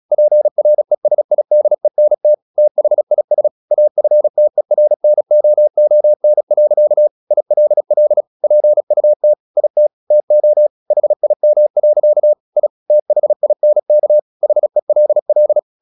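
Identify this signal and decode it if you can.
Morse code, heard as a keyed tone, about 36 wpm, 600 Hz. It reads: PRESIDENT THIS AFTERNOON. ILL PUT IT TO HIM. I THINK HELL
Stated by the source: Morse practice at 36 wpm, text of Ken Follett's novel Never, from CW med Gnister.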